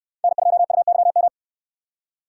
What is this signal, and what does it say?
Morse code, heard as a keyed tone, about 64 wpm, 700 Hz. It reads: N2UYM